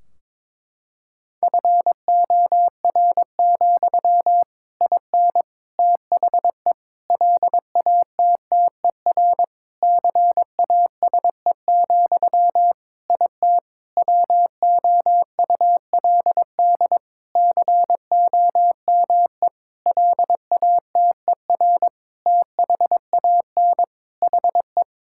Morse code, heard as a keyed tone, about 22 wpm, 700 Hz. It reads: FOR, IN THE LATTER CASE, IT WOULD COME LATER THAN HE
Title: Sentences from Top 1600 Words 22wpm